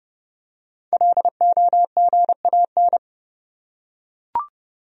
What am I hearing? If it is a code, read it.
Morse code, heard as a keyed tone, about 32 wpm, 700 Hz. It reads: LOGAN
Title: USA Common Names - 30wpm